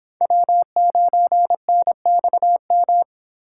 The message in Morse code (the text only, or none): W9NXM